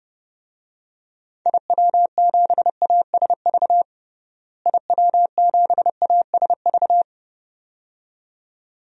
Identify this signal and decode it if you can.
Morse code, heard as a keyed tone, about 32 wpm, 700 Hz. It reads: IW7ASV IW7ASV
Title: DXCC-100 Call Signs - 6 Characters - Repeated 2x - 30wpm